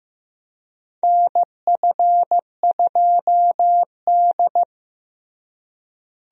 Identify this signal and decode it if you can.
Morse code, heard as a keyed tone, about 15 wpm, 700 Hz. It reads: NF2D
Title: USA Call Signs - 4 Characters - Repeated 3x - 15wpm